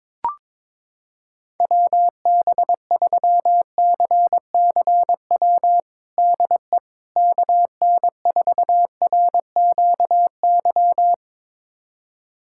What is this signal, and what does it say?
Morse code, heard as a keyed tone, about 22 wpm, 700 Hz. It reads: WB3CCW DE KN4RQY
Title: USA Callsign DE Callsign - 6 Characters - 22wpm